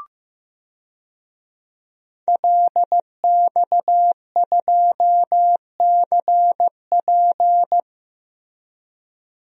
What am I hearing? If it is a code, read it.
Morse code, heard as a keyed tone, about 15 wpm, 700 Hz. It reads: LX2CP
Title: DXCC-100 Call Signs - 5 Characters - Repeated 2x - 15wpm